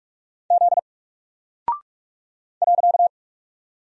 Morse code, heard as a keyed tone, about 44 wpm, 700 Hz. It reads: Z .
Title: Single Character (Letter:Number:Punctuation:Prosign) Alt 45wpm